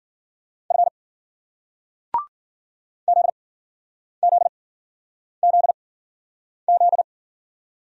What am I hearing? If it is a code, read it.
Morse code, heard as a keyed tone, about 52 wpm, spaced as for 9 wpm, 700 Hz. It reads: F ZZZZ